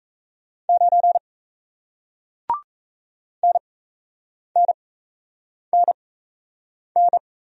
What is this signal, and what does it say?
Morse code, heard as a keyed tone, about 38 wpm, spaced as for 8 wpm, 700 Hz. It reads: 9 NNNN